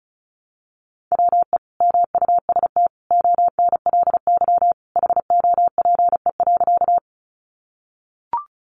Morse code, heard as a keyed tone, about 34 wpm, 700 Hz. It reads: WE MUST ONLY HOPE.
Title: Sentences from Top 1000 Words 35wpm